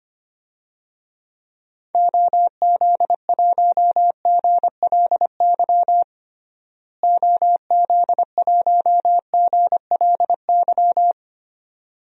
Morse code, heard as a keyed tone, about 25 wpm, 700 Hz. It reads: OZ1GLY OZ1GLY
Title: DXCC-100 Call Signs - 6 Characters - Repeated 2x - 25wpm